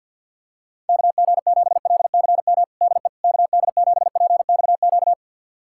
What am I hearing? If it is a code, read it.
Morse code, heard as a keyed tone, about 52 wpm, 700 Hz. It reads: KG7LCM DE KD7PXQ
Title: USA Callsign DE Callsign - 6 Characters - 50wpm